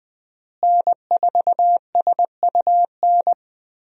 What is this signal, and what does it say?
Morse code, heard as a keyed tone, about 20 wpm, 700 Hz. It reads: N4SUN
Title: USA Call Signs - 5 Characters - Repeated 3x - 20wpm